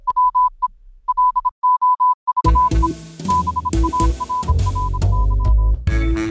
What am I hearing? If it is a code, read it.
Morse code, heard as a keyed tone, about 26 wpm, 1000 Hz. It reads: WE LOVE BARRK